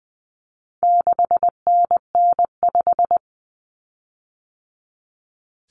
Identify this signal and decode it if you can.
Morse code, heard as a keyed tone, about 20 wpm, 700 Hz. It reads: 6NN5
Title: Common Cut Numbers - 4-Digit 20wpm